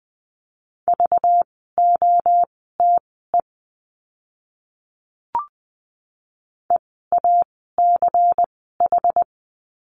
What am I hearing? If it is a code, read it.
Morse code, heard as a keyed tone, about 20 wpm, spaced as for 14 wpm, 700 Hz. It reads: VOTE EACH
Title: 192 Lesson 31f - Letter v - Words 20wpm